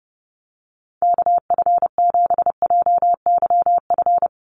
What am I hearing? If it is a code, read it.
Morse code, heard as a keyed tone, about 30 wpm, 700 Hz. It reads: KF7JYF
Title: USA Call Signs 30wpm